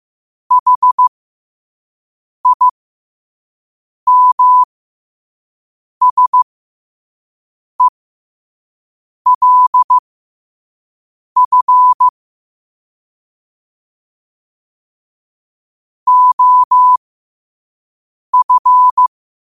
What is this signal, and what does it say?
Morse code, heard as a keyed tone, about 15 wpm, spaced as for 5 wpm, 1000 Hz. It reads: HIMSELF OF
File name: qotdfiles / 5wpm